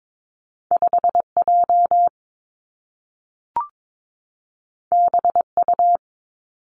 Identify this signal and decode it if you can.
Morse code, heard as a keyed tone, about 22 wpm, 700 Hz. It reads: 5 J B U
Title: Two Challenging Letter-Number Combinations 22wpm